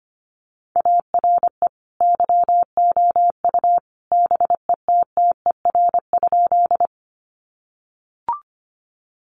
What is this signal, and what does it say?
Morse code, heard as a keyed tone, about 25 wpm, 700 Hz. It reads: ARE YOU BETTER?
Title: Sentences from Top 400 Words 25wpm